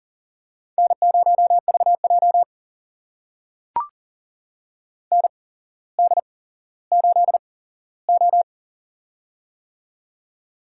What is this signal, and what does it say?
Morse code, heard as a keyed tone, about 38 wpm, spaced as for 17 wpm, 700 Hz. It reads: N0VJ ND8O
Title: USA Call Signs - 4 Characters - 8x Character Spacing - 40wpm